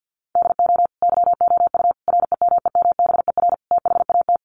Morse code, heard as a keyed tone, about 48 wpm, 700 Hz. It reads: DO YOU REMEMBER THAT
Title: Sentences from Top 1100 Words 50wpm